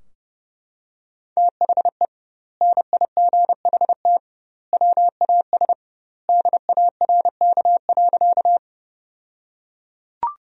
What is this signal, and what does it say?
Morse code, heard as a keyed tone, about 32 wpm, 700 Hz. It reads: THE NIGHT WAS DARK.
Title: 2x Word-Spacing - Sentences from Top 500 Words - 30wpm